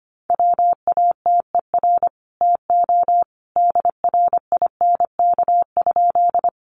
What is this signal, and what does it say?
Morse code, heard as a keyed tone, about 25 wpm, 700 Hz. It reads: WATER TO DRINK?
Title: Sentences from Top 1500 Words 25wpm